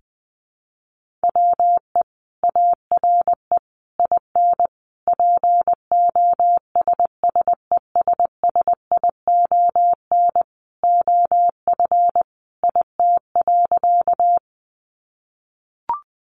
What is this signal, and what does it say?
Morse code, heard as a keyed tone, about 20 wpm, 700 Hz. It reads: WE ARE IN POSSESSION OF IT.
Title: Sentences from Top 1400 Words 20wpm